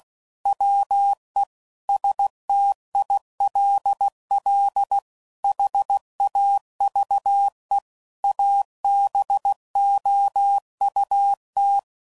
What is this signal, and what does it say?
Morse code, heard as a keyed tone, about 16 wpm, 800 Hz. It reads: WE STILL HAVE ABOUT
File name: CW Code Groups @15WPM / QSOs / WA7QOX